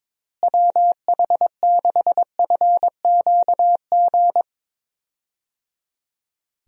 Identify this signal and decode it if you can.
Morse code, heard as a keyed tone, about 21 wpm, 700 Hz. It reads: WH6FQG
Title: USA Call Signs - 6 Characters - Repeated 3x - 22wpm